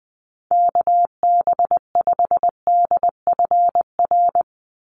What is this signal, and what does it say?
Morse code, heard as a keyed tone, about 20 wpm, 700 Hz. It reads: KB5DFR
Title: USA Call Signs 20wpm